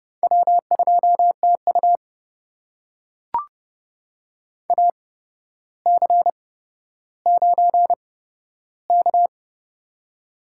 Morse code, heard as a keyed tone, about 31 wpm, spaced as for 13 wpm, 700 Hz. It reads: W2TU AC9K
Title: USA Call Signs - 4 Characters - 8x Character Spacing - 30wpm